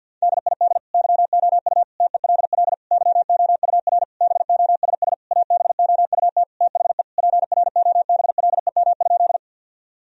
Morse code, heard as a keyed tone, about 50 wpm, 700 Hz. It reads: DID YOU TELL YOUR BOSS ABOUT THE PROBLEM?
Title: Sentences from Top 1000 Words - Review Part 4 - 50wpm